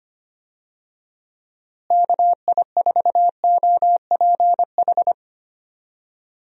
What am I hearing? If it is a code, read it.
Morse code, heard as a keyed tone, about 25 wpm, 700 Hz. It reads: KI4OPH
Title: USA Call Signs - 6 Characters - Repeated 2x - 25wpm